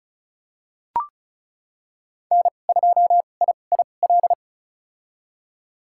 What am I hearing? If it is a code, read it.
Morse code, heard as a keyed tone, about 36 wpm, spaced as for 25 wpm, 700 Hz. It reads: N2IIL
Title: USA Call Signs - 5 Characters - 2x Character Spacing - 35wpm